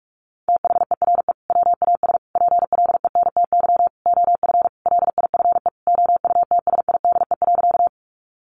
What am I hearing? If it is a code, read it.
Morse code, heard as a keyed tone, about 44 wpm, 700 Hz. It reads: THERE WAS PLENTY OF LIFE OUTSIDE.
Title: Sentences from Top 1300 Words 45wpm